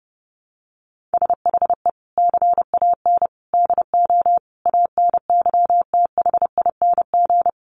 Morse code, heard as a keyed tone, about 30 wpm, 700 Hz. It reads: SHE CAN DO ANYTHING
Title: Sentences from Top 1000 Words 30wpm